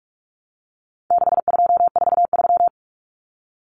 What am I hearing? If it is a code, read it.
Morse code, heard as a keyed tone, about 44 wpm, 700 Hz. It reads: 6243
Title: Numbers - 4-Digit 45wpm